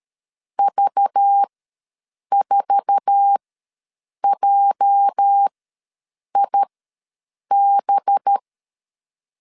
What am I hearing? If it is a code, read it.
Morse code, heard as a keyed tone, about 13 wpm, spaced as for 7 wpm, 800 Hz. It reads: V4JIB
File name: LEZIONI / 8 YJ45 / 8 YJ4 ws 12wpm 3249